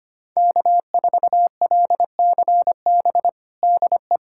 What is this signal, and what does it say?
Morse code, heard as a keyed tone, about 25 wpm, 700 Hz. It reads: K4LCB DE
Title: USA Callsign DE Callsign - 5 Characters - 25wpm